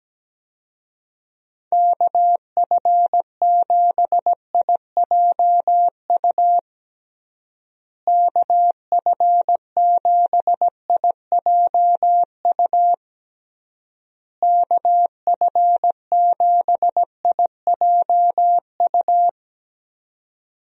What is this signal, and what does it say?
Morse code, heard as a keyed tone, about 17 wpm, 700 Hz. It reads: KF7IJU KF7IJU KF7IJU
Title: USA Call Signs - 6 Characters - Repeated 3x - 17wpm